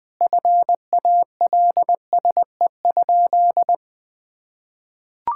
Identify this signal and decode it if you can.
Morse code, heard as a keyed tone, about 21 wpm, 700 Hz. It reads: FALSE?
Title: Sentences from Top 1600 Words 20wpm